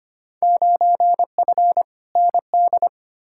Morse code, heard as a keyed tone, about 25 wpm, 700 Hz. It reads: 9F ND
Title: Field Day Exchanges Encore - 25wpm